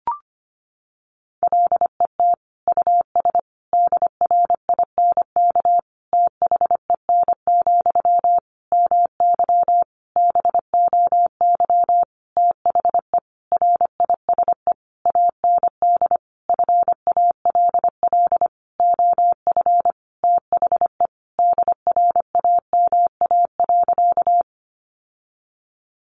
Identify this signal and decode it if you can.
Morse code, heard as a keyed tone, about 25 wpm, 700 Hz. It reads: LET US DRINK THEN, MY BOY THE RISE AND FALL OF THE DRAMA.
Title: Sentences from Top 1500 Words 25wpm